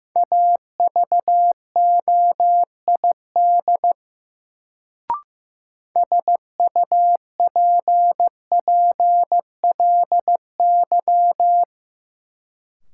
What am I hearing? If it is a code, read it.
Morse code, heard as a keyed tone, about 15 wpm, 700 Hz. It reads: AVOID SUPPLY
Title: Sentences from Top 1200 Words - New Word Drill Alt - 15wpm